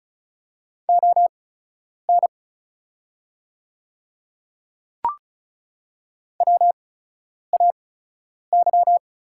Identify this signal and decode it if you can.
Morse code, heard as a keyed tone, about 36 wpm, spaced as for 10 wpm, 700 Hz. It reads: ON WAY
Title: 8x Character Spacing - Top 100 Words - 35wpm